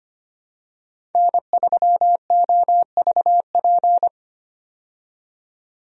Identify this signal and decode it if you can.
Morse code, heard as a keyed tone, about 25 wpm, 700 Hz. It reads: N3OVP
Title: USA Call Signs - 5 Characters - 25wpm